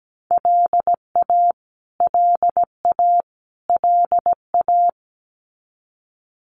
Morse code, heard as a keyed tone, about 17 wpm, 700 Hz. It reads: LA LA LA
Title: US States Repeated 3x 17wpm